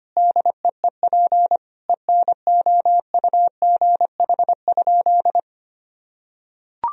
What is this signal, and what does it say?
Morse code, heard as a keyed tone, about 25 wpm, 700 Hz. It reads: DEEP ENOUGH?
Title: Sentences from Top 1000 Words - Review Part 3 - 25wpm